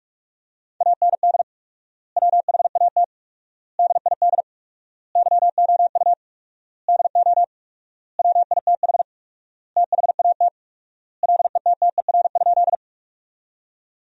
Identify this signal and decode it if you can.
Morse code, heard as a keyed tone, about 46 wpm, 700 Hz. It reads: AND WHAT DID YOU DO WITH THAT LETTER?
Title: 4x Word-Spacing - Sentences from Top 300 Words - 45wpm